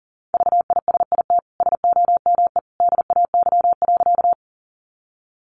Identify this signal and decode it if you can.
Morse code, heard as a keyed tone, about 40 wpm, 700 Hz. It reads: VISIT SOME DAY.
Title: Sentences from Top 900 Words 40wpm